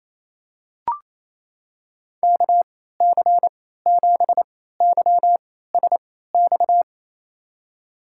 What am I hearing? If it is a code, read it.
Morse code, heard as a keyed tone, about 28 wpm, spaced as for 16 wpm, 700 Hz. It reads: KC7YSX
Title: USA Call Signs - 6 Characters - 3x Character Spacing - 28wpm